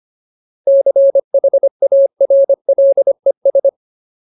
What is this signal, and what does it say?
Morse code, heard as a keyed tone, about 25 wpm, 550 Hz. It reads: CHARLES
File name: Ultimate Word List / 25WPM Ultimate